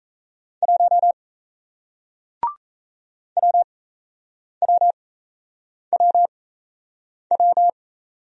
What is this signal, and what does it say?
Morse code, heard as a keyed tone, about 38 wpm, spaced as for 8 wpm, 700 Hz. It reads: 1 WWWW